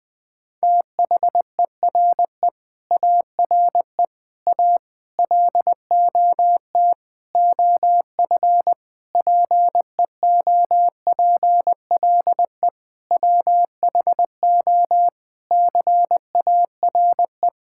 Morse code, heard as a keyed tone, about 20 wpm, 700 Hz. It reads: THERE ARE A LOT OF PEOPLE WHO CARE
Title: Sentences with 7-Word Phrases - Spoken Beginning - 20wpm